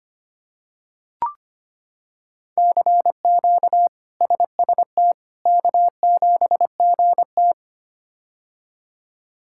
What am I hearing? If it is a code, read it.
Morse code, heard as a keyed tone, about 25 wpm, 700 Hz. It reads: CQ SST K7GT
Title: K1USN SST QSO Practice - 25wpm